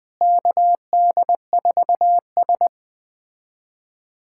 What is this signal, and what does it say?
Morse code, heard as a keyed tone, about 21 wpm, 700 Hz. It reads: KD4S
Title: USA Callsign DE Callsign - 4 Characters - 20wpm